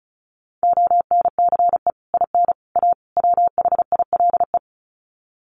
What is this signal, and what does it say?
Morse code, heard as a keyed tone, about 34 wpm, 700 Hz. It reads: ONCE IN A WHILE
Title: Sets of 4 Words 35wpm